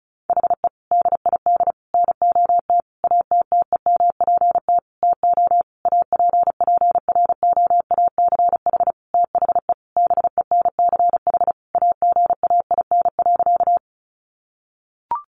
Morse code, heard as a keyed tone, about 34 wpm, 700 Hz. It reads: HE DID NOT ATTEMPT TO APPROACH THE BENCH AGAIN.
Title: Sentences from Top 1400 Words 35wpm